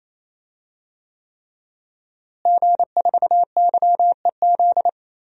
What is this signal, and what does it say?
Morse code, heard as a keyed tone, about 28 wpm, 700 Hz. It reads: G4YEZ
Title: DXCC-100 Call Signs - 5 Characters - Repeated 3x - 28wpm